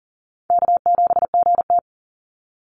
Morse code, heard as a keyed tone, about 40 wpm, 700 Hz. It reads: K7GT
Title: NA SOTA Chaser Callsigns 40wpm